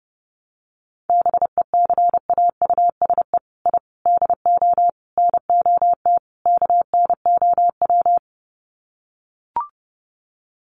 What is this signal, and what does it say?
Morse code, heard as a keyed tone, about 30 wpm, 700 Hz. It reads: BECAUSE I DO NOT KNOW
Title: Sets of 5 Words Encore2 30wpm